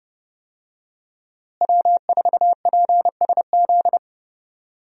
Morse code, heard as a keyed tone, about 31 wpm, 700 Hz. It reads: W4PSZ
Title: USA Call Signs - 5 Characters - Repeated 3x - 30wpm